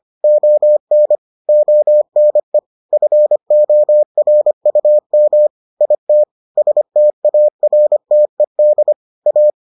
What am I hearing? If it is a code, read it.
Morse code, heard as a keyed tone, about 25 wpm, 600 Hz. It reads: ON ONE FORUM IT STARTED A